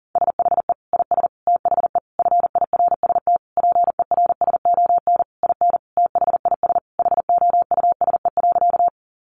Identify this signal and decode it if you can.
Morse code, heard as a keyed tone, about 40 wpm, 700 Hz. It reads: SHE IS THE FIRST PERSON IN THIS HOUSE.
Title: Sentences from Top 400 Words 40wpm